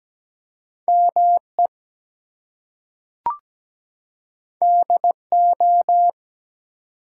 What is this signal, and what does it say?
Morse code, heard as a keyed tone, about 17 wpm, 700 Hz. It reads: ME DO
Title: Two Letter Words - 17wpm